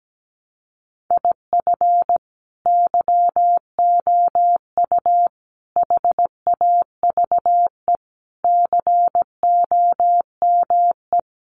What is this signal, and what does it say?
Morse code, heard as a keyed tone, about 17 wpm, 700 Hz. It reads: IF YOU HAVE COME
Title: Sentences from Top 800 Words 17wpm